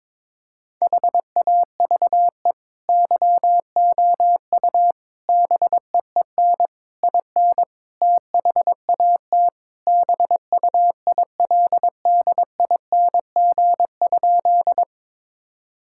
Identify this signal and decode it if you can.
Morse code, heard as a keyed tone, about 22 wpm, 700 Hz. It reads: HAVE YOU BEEN IN THAT BUILDING?
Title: Sentences from Top 1000 Words - Review Part 5 - 22wpm